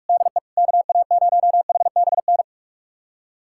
Morse code, heard as a keyed tone, about 46 wpm, 700 Hz. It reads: DE KA0HBN
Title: USA Callsign DE Callsign - 6 Characters - 45wpm